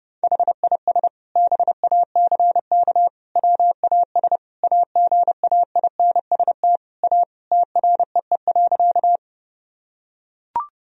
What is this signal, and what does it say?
Morse code, heard as a keyed tone, about 30 wpm, 700 Hz. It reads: HIS BACK WAS AGAINST A TREE.
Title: Sentences from Top 1000 Words - Review Part 1 - 30wpm